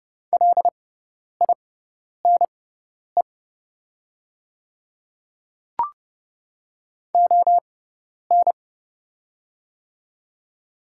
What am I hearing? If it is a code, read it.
Morse code, heard as a keyed tone, about 30 wpm, spaced as for 10 wpm, 700 Hz. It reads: LINE ON
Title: 6x Character Spacing - Top 500 Words - 30wpm